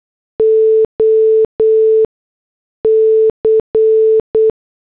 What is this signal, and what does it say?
Morse code, heard as a keyed tone, about 8 wpm, spaced as for 6 wpm, 440 Hz. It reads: OC
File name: morse / headlines